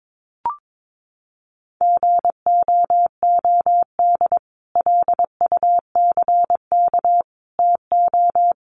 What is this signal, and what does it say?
Morse code, heard as a keyed tone, about 22 wpm, 700 Hz. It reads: GOOD LUCK TO
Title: Sentences from Top 1500 Words 22wpm